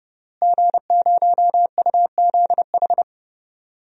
Morse code, heard as a keyed tone, about 30 wpm, 700 Hz. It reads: G0UZH
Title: DXCC-100 Call Signs - 5 Characters - Repeated 3x - 30wpm